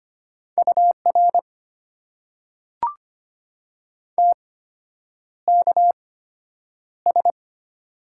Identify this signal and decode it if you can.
Morse code, heard as a keyed tone, about 25 wpm, spaced as for 9 wpm, 690 Hz. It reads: UR TKS
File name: Top 90 QSO Elements - 8x Character Spacing - 25wpm